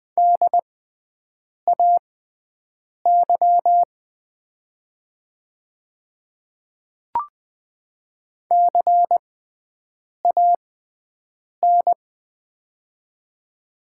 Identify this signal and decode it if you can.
Morse code, heard as a keyed tone, about 20 wpm, spaced as for 7 wpm, 700 Hz. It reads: DAY CAN
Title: 6x Character Spacing - Top 200 Words - 20wpm